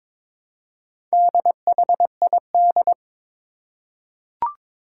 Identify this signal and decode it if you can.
Morse code, heard as a keyed tone, about 22 wpm, 700 Hz. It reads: DHID
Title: Four Challenging Letter-Number Combinations 22wpm